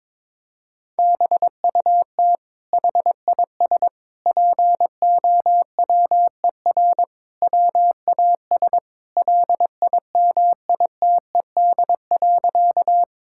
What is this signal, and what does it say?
Morse code, heard as a keyed tone, about 22 wpm, 700 Hz. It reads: BUT HIS POWER WAS LIMITED.
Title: Sentences from Top 1000 Words - Review Part 6 - 22wpm